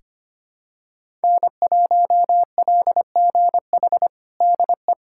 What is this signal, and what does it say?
Morse code, heard as a keyed tone, about 25 wpm, 700 Hz. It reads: N1LGH DE
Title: USA Callsign DE Callsign - 5 Characters - 25wpm